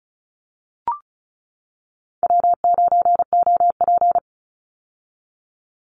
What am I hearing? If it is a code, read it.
Morse code, heard as a keyed tone, about 36 wpm, 700 Hz. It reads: W9OP
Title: Call Signs - 35wpm